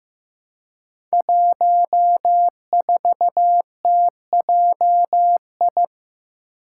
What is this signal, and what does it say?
Morse code, heard as a keyed tone, about 15 wpm, 700 Hz. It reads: 14TJI
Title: Five Challenging Letter-Number Combinations 15wpm